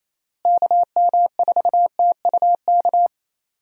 Morse code, heard as a keyed tone, about 28 wpm, 700 Hz. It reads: KM4TUK
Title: USA Call Signs - 6 Characters - Repeated 3x - 28wpm